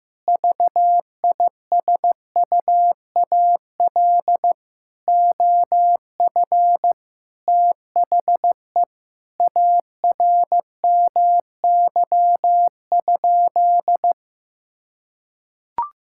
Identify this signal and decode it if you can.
Morse code, heard as a keyed tone, about 15 wpm, 700 Hz. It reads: VISUAL OF THE ARMY?
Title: Sentences from Top 1000 Words - Review Part 6 - 15wpm